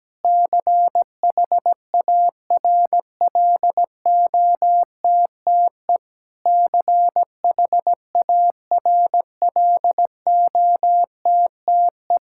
Morse code, heard as a keyed tone, about 17 wpm, 700 Hz. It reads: CHARLOTTE CHARLOTTE
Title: Practice QSO - Exchange 1 - Beginning - 17wpm